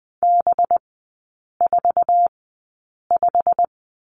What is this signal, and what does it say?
Morse code, heard as a keyed tone, about 20 wpm, spaced as for 8 wpm, 700 Hz. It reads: B45
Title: 191 Lesson 31e - Letter v - Three Characters 20wpm